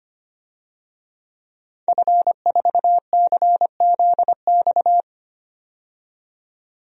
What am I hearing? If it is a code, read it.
Morse code, heard as a keyed tone, about 25 wpm, 700 Hz. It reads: F4CZX
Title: DXCC-100 Call Signs - 5 Characters - 25wpm